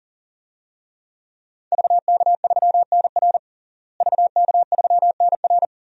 Morse code, heard as a keyed tone, about 40 wpm, 700 Hz. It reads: VK3NR VK3NR
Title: DXCC-100 Call Signs - 5 Characters - Repeated 3x - 40wpm